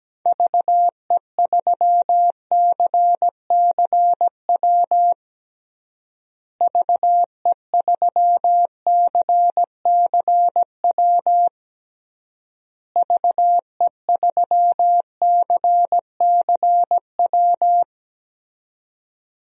Morse code, heard as a keyed tone, about 17 wpm, 700 Hz. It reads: VE3CCW VE3CCW VE3CCW
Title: DXCC-100 Call Signs - 6 Characters - Repeated 3x - 17wpm